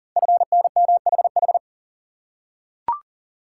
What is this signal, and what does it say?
Morse code, heard as a keyed tone, about 38 wpm, 700 Hz. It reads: FNMHH
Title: Code Groups - 5 Letters - 40wpm